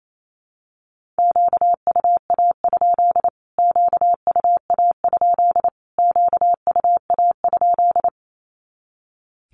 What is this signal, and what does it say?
Morse code, heard as a keyed tone, about 28 wpm, 700 Hz. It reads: QUA? QUA? QUA?
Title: Q-Codes Repeated 3x 28wpm